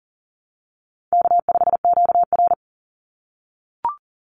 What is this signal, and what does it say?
Morse code, heard as a keyed tone, about 40 wpm, 700 Hz. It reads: K5QR
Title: NA SOTA Chaser Callsigns 40wpm